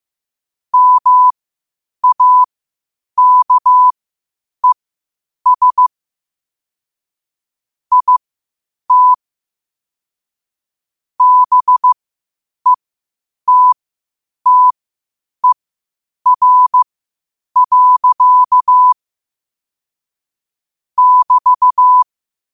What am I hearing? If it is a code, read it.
Morse code, heard as a keyed tone, about 15 wpm, spaced as for 8 wpm, 1000 Hz. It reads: MAKES IT BETTER. <BT>
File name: qotdfiles / 8wpm